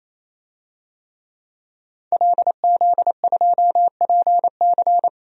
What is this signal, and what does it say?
Morse code, heard as a keyed tone, about 27 wpm, 700 Hz. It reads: LZ2PC
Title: DXCC-100 Call Signs - 5 Characters - 28wpm